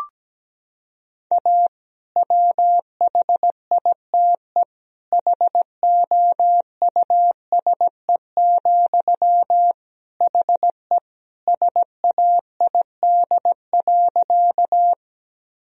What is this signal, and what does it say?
Morse code, heard as a keyed tone, about 17 wpm, 700 Hz. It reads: A WHITE HOUSE, HE SAID.
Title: Sentences from Top 1000 Words - Review Part 1 - 17wpm